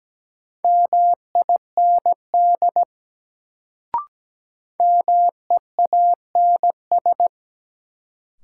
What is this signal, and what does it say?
Morse code, heard as a keyed tone, about 17 wpm, 700 Hz. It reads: MIND MEANS
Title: Sets of 5 Words - New Word Drill - 17wpm